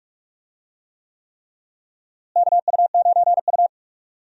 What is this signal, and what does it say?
Morse code, heard as a keyed tone, about 43 wpm, 700 Hz. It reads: KU9U
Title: USA Call Signs - 4 Characters - Repeated 3x - 45wpm